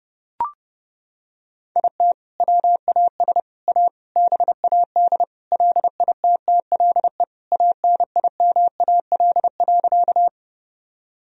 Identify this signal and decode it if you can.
Morse code, heard as a keyed tone, about 30 wpm, 700 Hz. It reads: IT WAS A BAD LITTLE ANIMAL.
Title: Sentences from Top 1000 Words - Review Part 1 - 30wpm